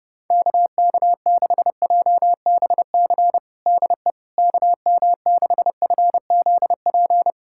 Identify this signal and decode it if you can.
Morse code, heard as a keyed tone, about 30 wpm, 700 Hz. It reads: KK6JBC DE KM6FZP
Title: USA Callsign DE Callsign - 6 Characters - 30wpm